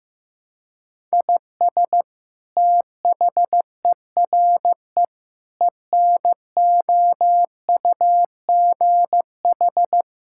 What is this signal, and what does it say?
Morse code, heard as a keyed tone, about 16 wpm, 700 Hz. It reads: IS THERE ENOUGH
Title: Sentences from Top 1000 Words - Review Part 5 - 15wpm